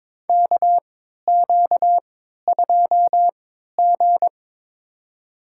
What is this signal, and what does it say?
Morse code, heard as a keyed tone, about 22 wpm, spaced as for 12 wpm, 700 Hz. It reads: KQ2G
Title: USA Call Signs - 4 Characters - 3x Character Spacing - 22wpm